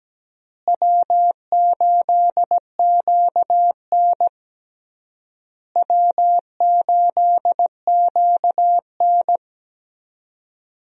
Morse code, heard as a keyed tone, about 17 wpm, 700 Hz. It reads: W8QN W8QN
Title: USA Call Signs - 4 Characters - Repeated 3x - 17wpm